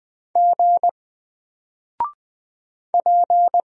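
Morse code, heard as a keyed tone, about 20 wpm, 700 Hz. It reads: G P
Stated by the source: Single letter and number Morse drill at 20 wpm, from N4LCD.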